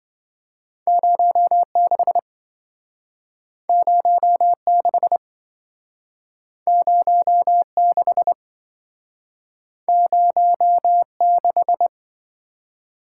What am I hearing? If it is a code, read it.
Morse code, heard as a keyed tone, about 27 wpm, 700 Hz. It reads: 06 06 06 06